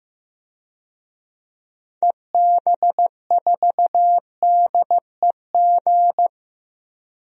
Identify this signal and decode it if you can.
Morse code, heard as a keyed tone, about 16 wpm, 700 Hz. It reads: EB4DEG
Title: DXCC-100 Call Signs - 6 Characters - Repeated 2x - 15wpm